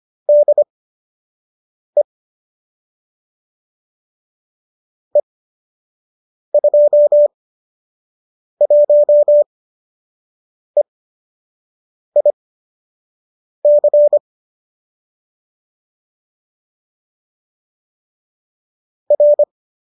Morse code, heard as a keyed tone, about 25 wpm, spaced as for 6 wpm, 590 Hz. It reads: DE E21EIC R